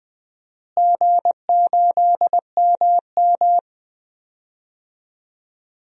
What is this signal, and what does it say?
Morse code, heard as a keyed tone, about 20 wpm, 700 Hz. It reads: G8MM
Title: DXCC-100 Call Signs - 4 Characters - Repeated 3x - 20wpm